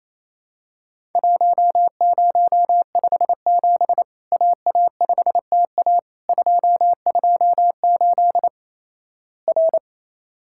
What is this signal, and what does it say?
Morse code, heard as a keyed tone, about 27 wpm, 700 Hz. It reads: 1057 AA5TA 228 R
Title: Worked All Europe DX Contest - QTC Exchange - 28wpm